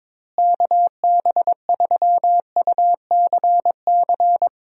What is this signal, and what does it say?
Morse code, heard as a keyed tone, about 22 wpm, 700 Hz. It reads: KB3UCC
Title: USA Call Signs - 6 Characters - Repeated 3x - 22wpm